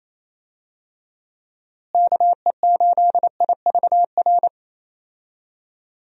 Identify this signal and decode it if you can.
Morse code, heard as a keyed tone, about 27 wpm, 700 Hz. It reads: KE8IVR
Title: USA Call Signs - 6 Characters - Repeated 2x - 28wpm